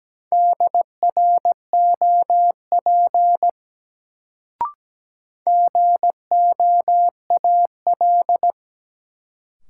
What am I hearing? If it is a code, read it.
Morse code, heard as a keyed tone, about 17 wpm, 700 Hz. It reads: DROP GOAL
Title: Sentences from Top 1400 Words - New Word Drill Alt - 17wpm